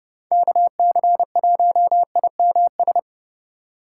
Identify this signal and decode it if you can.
Morse code, heard as a keyed tone, about 27 wpm, 700 Hz. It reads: KC1IMS